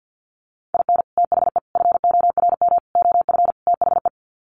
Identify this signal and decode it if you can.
Morse code, heard as a keyed tone, about 50 wpm, 700 Hz. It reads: IN THE FORM OF THE
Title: Sets of 5 Words Encore2 50wpm